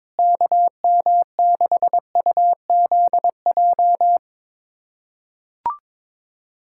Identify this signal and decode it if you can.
Morse code, heard as a keyed tone, about 21 wpm, 700 Hz. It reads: KM6UZJ
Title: USA Callsign DE Callsign - 6 Characters - 22wpm